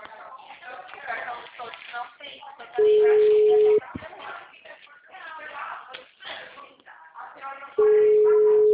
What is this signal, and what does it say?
Ringback tone (Europe, 425 Hz), with speech.
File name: out- -2001- - - .51738